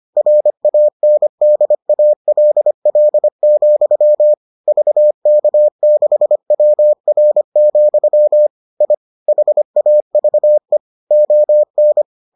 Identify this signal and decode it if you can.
Morse code, heard as a keyed tone, about 25 wpm, 600 Hz. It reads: RANDALL, VK6WR, I HAVE ON